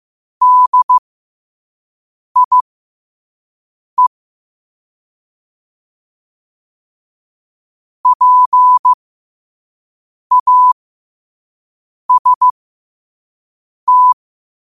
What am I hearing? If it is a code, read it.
Morse code, heard as a keyed tone, about 15 wpm, spaced as for 5 wpm, 1000 Hz. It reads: DIE PAST
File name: qotdfiles / 5wpm